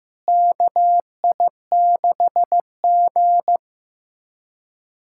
Morse code, heard as a keyed tone, about 16 wpm, 700 Hz. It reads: KI6G
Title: USA Callsign DE Callsign - 4 Characters - 15wpm